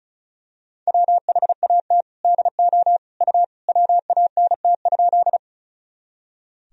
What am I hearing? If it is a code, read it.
Morse code, heard as a keyed tone, about 36 wpm, 700 Hz. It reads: WHAT DO U WANT?
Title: Sentences from Top 200 Words and Abbreviations 35wpm